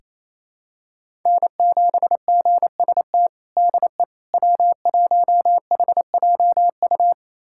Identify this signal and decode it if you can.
Morse code, heard as a keyed tone, about 28 wpm, 700 Hz. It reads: N7GST DE W1HJU
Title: USA Callsign DE Callsign - 5 Characters - 28wpm